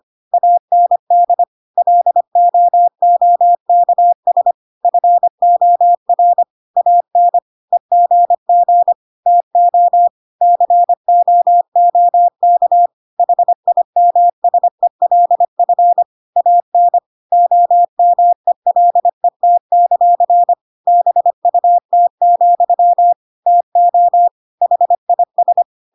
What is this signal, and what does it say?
Morse code, heard as a keyed tone, about 25 wpm, 700 Hz. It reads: AND LOOKS FOR AN EGG TO COOK HIMSELF AN OMELET; BUT, TO HIS
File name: Pinocchio / Pinocchio ch5 25 25 ik1hge